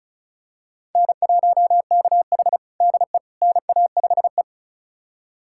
Morse code, heard as a keyed tone, about 34 wpm, 700 Hz. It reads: N1KH DE NA5E
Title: USA Callsign DE Callsign - 4 Characters - 35wpm